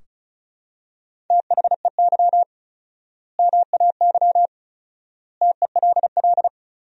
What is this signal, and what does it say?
Morse code, heard as a keyed tone, about 34 wpm, 700 Hz. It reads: THEY MAY TELL
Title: 4x Word-Spacing - Sentences from Top 300 Words - 35wpm